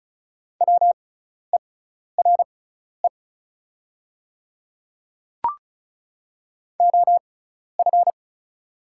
Morse code, heard as a keyed tone, about 34 wpm, spaced as for 12 wpm, 700 Hz. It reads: WERE OF
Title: 6x Character Spacing - Top 100 Words - 35wpm